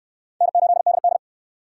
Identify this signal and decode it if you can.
Morse code, heard as a keyed tone, about 59 wpm, 700 Hz. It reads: N8DD